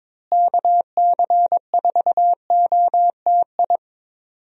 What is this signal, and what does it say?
Morse code, heard as a keyed tone, about 22 wpm, 700 Hz. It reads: KC4OTI
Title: USA Call Signs - 6 Characters - Repeated 2x - 22wpm